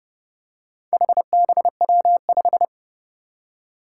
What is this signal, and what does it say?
Morse code, heard as a keyed tone, about 30 wpm, 700 Hz. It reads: HBW5
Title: Four Challenging Letter-Number Combinations 30wpm